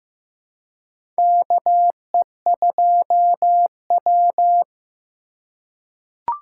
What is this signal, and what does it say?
Morse code, heard as a keyed tone, about 15 wpm, 700 Hz. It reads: KE2W
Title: USA Call Signs - 3x Character Spacing - 15wpm